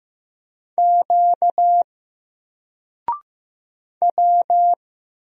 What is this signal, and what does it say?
Morse code, heard as a keyed tone, about 15 wpm, 700 Hz. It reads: Q W
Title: Single Character (Letter:Number:Punctuation:Prosign) 15wpm